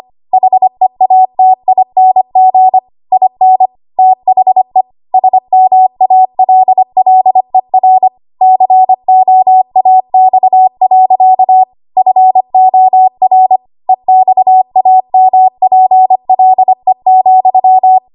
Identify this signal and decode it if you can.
Morse code, heard as a keyed tone, about 25 wpm, 750 Hz. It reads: HEATING IN THE SMALLER COAX. FOR EXAMPLE,
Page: W1AW Web Code Practice Files